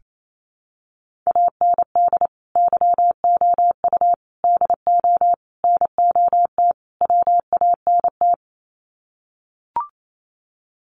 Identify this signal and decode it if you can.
Morse code, heard as a keyed tone, about 28 wpm, 700 Hz. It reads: AND YOU DO NOT WANT
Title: Sets of 5 Words Encore2 28wpm